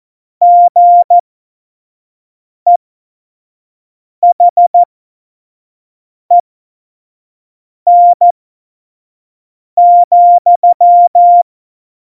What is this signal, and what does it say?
Morse code, heard as a keyed tone, about 14 wpm, spaced as for 5 wpm, 700 Hz. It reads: GEHEN,
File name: Morsekurs / Witze / Witz 28 April 2020, 25 F70